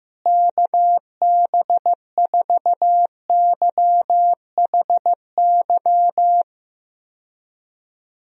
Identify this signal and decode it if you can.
Morse code, heard as a keyed tone, about 15 wpm, 700 Hz. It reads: KB4YHY
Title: USA Call Signs - 6 Characters - Repeated 3x - 15wpm